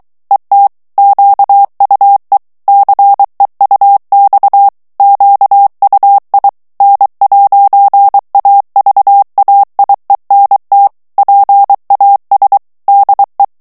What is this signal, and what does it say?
Morse code, heard as a keyed tone, about 23 wpm, 800 Hz. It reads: ET QUE CEUX QUI N'AVAIENT PAS DE
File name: morse mp3 / FRANCAIS TEXTE CLAIR 2 VITESSE 22 MOTS MINUTES